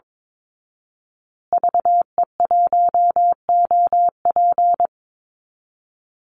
Morse code, heard as a keyed tone, about 22 wpm, 700 Hz. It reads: VE1OP
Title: Call Signs Repeated 2x - 22wpm